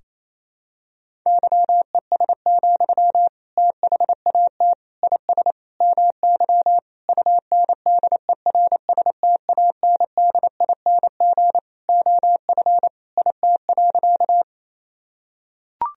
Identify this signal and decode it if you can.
Morse code, heard as a keyed tone, about 28 wpm, 700 Hz. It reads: YES, THAT IS MY UNDERSTANDING OF IT.
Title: Sentences from Top 1000 Words - Review Part 3 - 28wpm